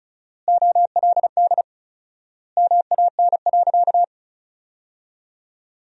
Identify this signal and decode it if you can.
Morse code, heard as a keyed tone, about 36 wpm, 700 Hz. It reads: OLD MAN.
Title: 4x Word-Spacing - Sentences from Top 500 Words - 35wpm